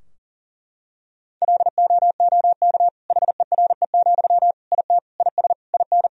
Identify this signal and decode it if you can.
Morse code, heard as a keyed tone, about 40 wpm, 700 Hz. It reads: LOOK HERE, IT IS IN
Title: Sentences from Top 1000 Words - Review Part 2 - 40wpm